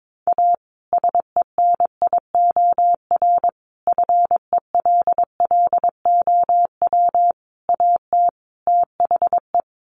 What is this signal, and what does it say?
Morse code, heard as a keyed tone, about 22 wpm, 700 Hz. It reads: A SENIOR FELLOW AT THE
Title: Sets of 5 Words Encore 22wpm